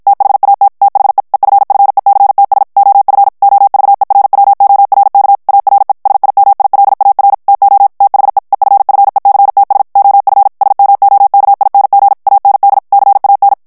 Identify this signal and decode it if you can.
Morse code, heard as a keyed tone, about 51 wpm, 800 Hz. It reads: THAT THE EFFECTS OF OVERWORK ARE SIMILAR TO THE EFFECTS OF SMOKING AND CAN
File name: morse mp3 / ANGLAIS texte clair 1 vitesse 50 mots minute